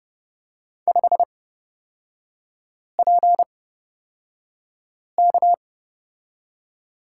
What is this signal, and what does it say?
Morse code, heard as a keyed tone, about 30 wpm, spaced as for 5 wpm, 700 Hz. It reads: 5PK
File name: Lesson 36e - BT - Three Characters 30wpm